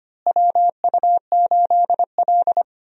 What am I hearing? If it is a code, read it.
Morse code, heard as a keyed tone, about 25 wpm, 700 Hz. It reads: WU8L
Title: USA Call Signs - 4 Characters - Repeated 3x - 25wpm